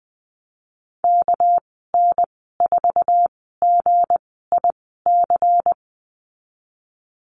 Morse code, heard as a keyed tone, about 19 wpm, spaced as for 14 wpm, 700 Hz. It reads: KN4GIC
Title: 207 Lesson 33g - Letter j - Call Signs 20wpm